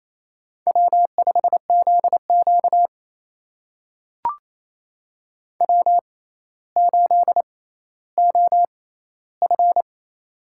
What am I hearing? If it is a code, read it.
Morse code, heard as a keyed tone, about 28 wpm, spaced as for 14 wpm, 700 Hz. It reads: W5ZQ W8OF
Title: USA Call Signs - 4 Characters - 6x Character Spacing - 28wpm